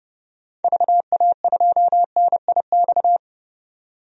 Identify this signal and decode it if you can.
Morse code, heard as a keyed tone, about 29 wpm, 700 Hz. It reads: VA2NIX
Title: DXCC-100 Call Signs - 6 Characters - 30wpm